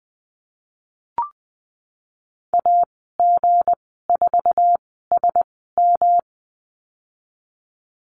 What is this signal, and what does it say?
Morse code, heard as a keyed tone, about 20 wpm, spaced as for 15 wpm, 700 Hz. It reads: AG4SM
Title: 161 Lesson 26g - Letter g - Call Signs 20wpm